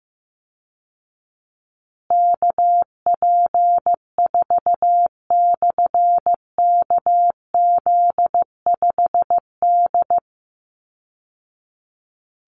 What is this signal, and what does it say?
Morse code, heard as a keyed tone, about 15 wpm, 700 Hz. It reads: KP4/KZ5D
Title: Call Signs Repeated 3x - 15wpm